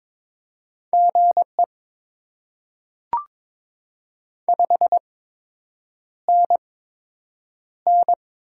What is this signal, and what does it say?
Morse code, heard as a keyed tone, about 22 wpm, spaced as for 8 wpm, 700 Hz. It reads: GE 5NN
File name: Top 20 QSO Elements - 8x Character Spacing - 22wpm